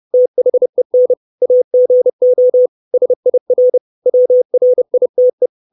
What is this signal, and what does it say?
Morse code, heard as a keyed tone, about 30 wpm, 500 Hz. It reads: THEN AGO SIR WRITE
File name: 632 Words 30WPM